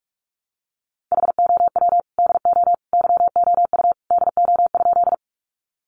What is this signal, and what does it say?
Morse code, heard as a keyed tone, about 44 wpm, 700 Hz. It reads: HOW DO YOU DO?
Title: Sentences from Top 100 Words 45wpm